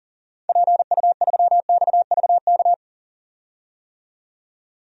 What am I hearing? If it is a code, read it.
Morse code, heard as a keyed tone, about 40 wpm, 700 Hz. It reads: PU3XVK
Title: DXCC-100 Call Signs - 6 Characters - 40wpm